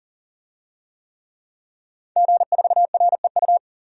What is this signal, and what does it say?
Morse code, heard as a keyed tone, about 40 wpm, 700 Hz. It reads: G4REU
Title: DXCC-100 Call Signs - 5 Characters - 40wpm